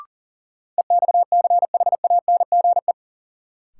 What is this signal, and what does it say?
Morse code, heard as a keyed tone, about 40 wpm, 700 Hz. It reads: EXCHANGE
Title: Sentences from Top 1000 Words - Review Part 6 - New Word Drill - 40wpm